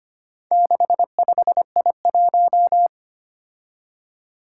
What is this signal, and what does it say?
Morse code, heard as a keyed tone, about 25 wpm, 700 Hz. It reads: 65I1
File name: practice / 25wpm / Four Letter-Number Combinations - 25wpm